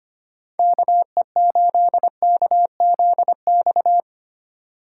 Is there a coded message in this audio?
Yes: Morse code from 0.6 to 4.0 s, heard as a keyed tone, about 32 wpm, 700 Hz. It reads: KE8KZX